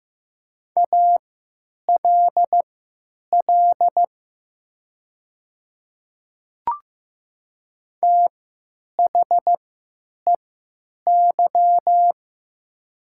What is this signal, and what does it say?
Morse code, heard as a keyed tone, about 15 wpm, spaced as for 9 wpm, 700 Hz. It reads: ALL THEY
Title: 3x Character Spacing - Top 200 Words - 15wpm